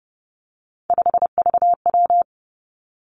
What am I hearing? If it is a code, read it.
Morse code, heard as a keyed tone, about 30 wpm, 700 Hz. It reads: 5VW
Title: Three Challenging Letter-Number Combinations 30wpm